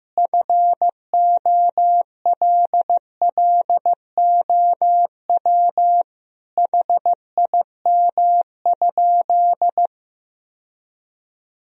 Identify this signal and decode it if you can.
Morse code, heard as a keyed tone, about 15 wpm, 700 Hz. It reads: FOLLOW HIM?
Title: Sentences from Top 1000 Words - Review Part 1 - 15wpm